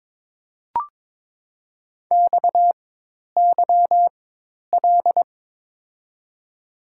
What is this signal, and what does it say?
Morse code, heard as a keyed tone, about 22 wpm, spaced as for 10 wpm, 700 Hz. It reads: XYL
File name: Top 90 QSO Elements - 4x Character Spacing - 22wpm